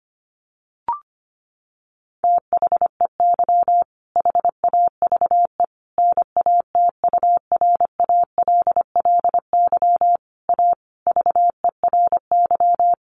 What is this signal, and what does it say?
Morse code, heard as a keyed tone, about 25 wpm, 700 Hz. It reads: THEY HAVE NATURALLY A VERY
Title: Sentences from Top 1300 Words 25wpm